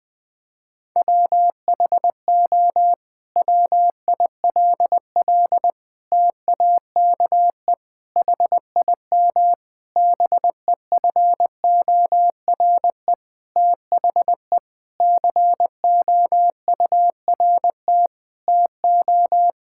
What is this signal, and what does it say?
Morse code, heard as a keyed tone, about 20 wpm, 700 Hz. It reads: WHO WILL TAKE HIM BEFORE THE COURT TO
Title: Sentences from Top 1600 Words 20wpm